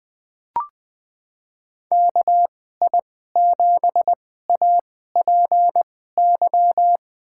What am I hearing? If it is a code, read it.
Morse code, heard as a keyed tone, about 20 wpm, spaced as for 14 wpm, 700 Hz. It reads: KI7APY
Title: USA Call Signs - 6 Characters - 2x Character Spacing - 20wpm